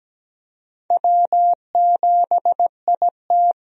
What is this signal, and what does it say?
Morse code, heard as a keyed tone, about 17 wpm, 700 Hz. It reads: W7IT
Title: USA Call Signs - 4 Characters - Repeated 3x - 17wpm